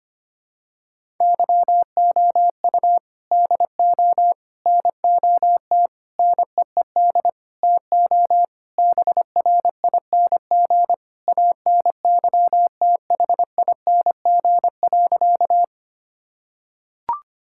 Morse code, heard as a keyed tone, about 25 wpm, 700 Hz. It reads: YOU DO NOT NEED TO BRING ANYTHING.
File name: practice / 25wpm / Sentences with 4-Word Phrases - 25wpm